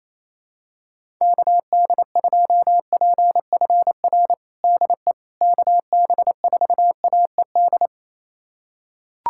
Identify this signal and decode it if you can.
Morse code, heard as a keyed tone, about 27 wpm, 700 Hz. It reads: KD2PFR DE KB4AED
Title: USA Callsign DE Callsign - 6 Characters - 28wpm